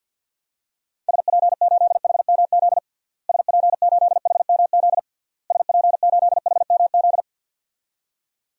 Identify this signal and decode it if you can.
Morse code, heard as a keyed tone, about 47 wpm, 700 Hz. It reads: SP8HMZ SP8HMZ SP8HMZ
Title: DXCC-100 Call Signs - 6 Characters - Repeated 3x - 50wpm